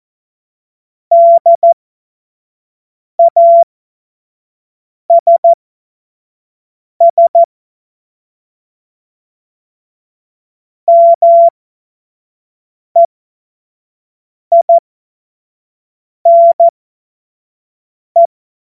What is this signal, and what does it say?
Morse code, heard as a keyed tone, about 14 wpm, spaced as for 5 wpm, 680 Hz. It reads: DASS MEINE
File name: Morsekurs / Witze / Witz 26 März 2024 25 F70